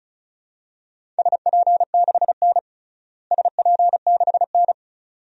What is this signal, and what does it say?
Morse code, heard as a keyed tone, about 34 wpm, 700 Hz. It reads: SP6N SP6N
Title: DXCC-100 Call Signs - 4 Characters - Repeated 2x - 35wpm